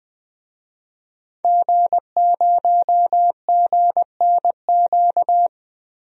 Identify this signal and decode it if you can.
Morse code, heard as a keyed tone, about 20 wpm, 700 Hz. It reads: G0GNQ
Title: DXCC-100 Call Signs - 5 Characters - 20wpm